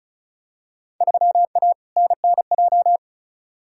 Morse code, heard as a keyed tone, about 36 wpm, 700 Hz. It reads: 3A NNJ
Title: Field Day Exchanges - 35wpm